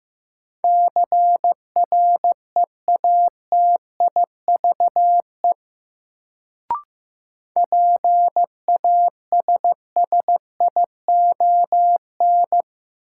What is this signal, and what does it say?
Morse code, heard as a keyed tone, about 15 wpm, 700 Hz. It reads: CREATIVE PASSION
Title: Sentences from Top 1300 Words - New Word Drill Alt - 15wpm